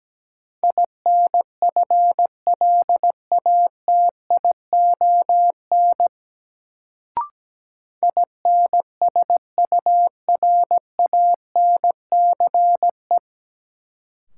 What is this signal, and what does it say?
Morse code, heard as a keyed tone, about 17 wpm, 700 Hz. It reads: INFLATION INSURANCE
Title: Sentences from Top 1100 Words - New Word Drill Alt - 17wpm